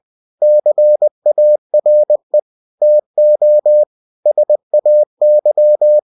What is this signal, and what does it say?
Morse code, heard as a keyed tone, about 20 wpm, 600 Hz. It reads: CARE TO SAY
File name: cw-books / winnie-the-pooh / winnie-20wpm-page-0065